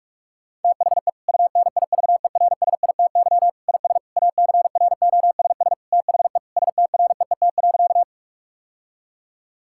Morse code, heard as a keyed tone, about 46 wpm, 700 Hz. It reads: THE UNIVERSITY IS ACROSS THE STREET.
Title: Sentences from Top 1000 Words - Review Part 5 - 45wpm